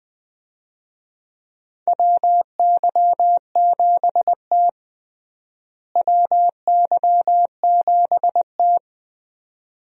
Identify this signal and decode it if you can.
Morse code, heard as a keyed tone, about 20 wpm, 700 Hz. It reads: WY7T WY7T
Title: USA Call Signs - 4 Characters - Repeated 2x - 20wpm